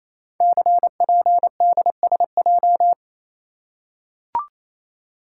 Morse code, heard as a keyed tone, about 27 wpm, 700 Hz. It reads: CPDSJ
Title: Code Groups - 5 Letter:Numbers - 28wpm